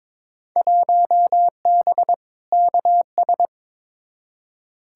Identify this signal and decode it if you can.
Morse code, heard as a keyed tone, about 22 wpm, 700 Hz. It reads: 1B KS
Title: Field Day Exchanges Encore - 22wpm